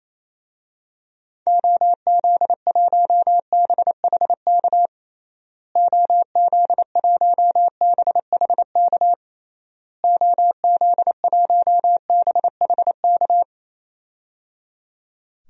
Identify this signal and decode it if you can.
Morse code, heard as a keyed tone, about 28 wpm, 700 Hz. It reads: OZ1BHK OZ1BHK OZ1BHK
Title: DXCC-100 Call Signs - 6 Characters - Repeated 3x - 28wpm